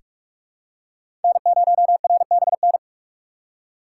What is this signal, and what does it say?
Morse code, heard as a keyed tone, about 45 wpm, 700 Hz. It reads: N0RBN
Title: USA Call Signs - 5 Characters - 4x Character Spacing - 45wpm